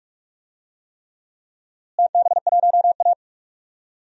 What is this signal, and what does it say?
Morse code, heard as a keyed tone, about 44 wpm, 700 Hz. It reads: TB1A
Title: DXCC-100 Call Signs - 4 Characters - Repeated 3x - 45wpm